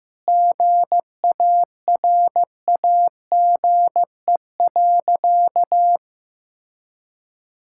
Morse code, heard as a keyed tone, about 16 wpm, 700 Hz. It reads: GARAGE.
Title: Sentences from Top 1600 Words 15wpm